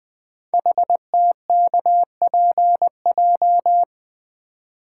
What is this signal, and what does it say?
Morse code, heard as a keyed tone, about 20 wpm, 700 Hz. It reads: HTKPJ
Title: Code Groups - 5 Letter:Numbers - 20wpm